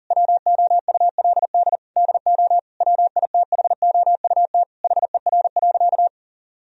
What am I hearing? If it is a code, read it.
Morse code, heard as a keyed tone, about 40 wpm, 700 Hz. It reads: WOULD DO WITHOUT HER.
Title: Sentences from Top 1000 Words - Review Part 3 - 40wpm